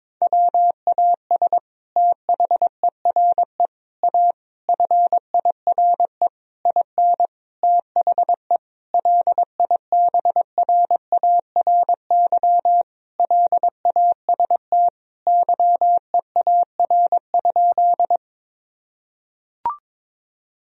Morse code, heard as a keyed tone, about 22 wpm, 700 Hz. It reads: WAS THERE A FIRE IN THE LIBRARY LAST YEAR?
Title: Sentences from Top 1000 Words - Review Part 4 - 22wpm